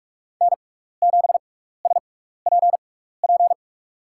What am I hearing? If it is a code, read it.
Morse code, heard as a keyed tone, about 41 wpm, spaced as for 15 wpm, 700 Hz. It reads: N7SPP
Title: USA Call Signs - 6x Character Spacing - 45wpm